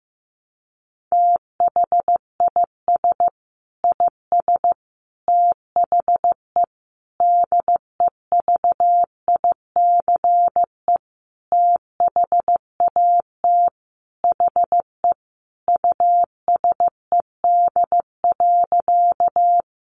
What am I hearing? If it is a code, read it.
Morse code, heard as a keyed tone, about 15 wpm, 700 Hz. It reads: THIS IS THE DEVICE THAT HE USED.
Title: Sentences from Top 1500 Words 15wpm